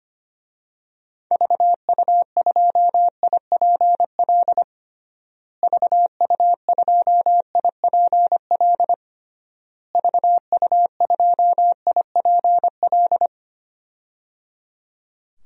Morse code, heard as a keyed tone, about 25 wpm, 700 Hz. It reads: VU2IPL VU2IPL VU2IPL
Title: DXCC-100 Call Signs - 6 Characters - Repeated 3x - 25wpm